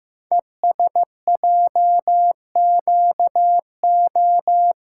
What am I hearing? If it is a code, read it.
Morse code, heard as a keyed tone, about 15 wpm, 700 Hz. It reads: ESJQO
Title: Code Groups - 5 Letters - 15wpm